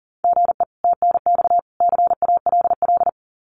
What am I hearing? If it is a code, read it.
Morse code, heard as a keyed tone, about 40 wpm, 700 Hz. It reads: GE TNX CALL
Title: Common QSO Phrases 40wpm